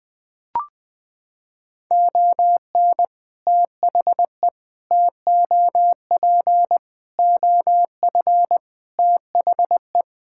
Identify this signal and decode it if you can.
Morse code, heard as a keyed tone, about 20 wpm, 700 Hz. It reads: ON THE TOP OF THE
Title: Sentences from Top 1000 Words - Review Part 6 - 20wpm